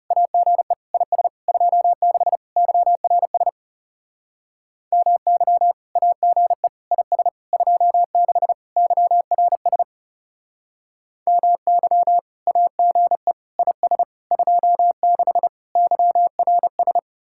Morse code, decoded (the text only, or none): AGE IS 26 YRS MY AGE IS 26 YRS MY AGE IS 26 YRS